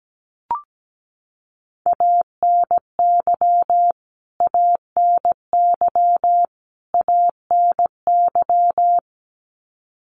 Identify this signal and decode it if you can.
Morse code, heard as a keyed tone, about 17 wpm, 700 Hz. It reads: ANY ANY ANY
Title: Top 500 Words Repeated 17wpm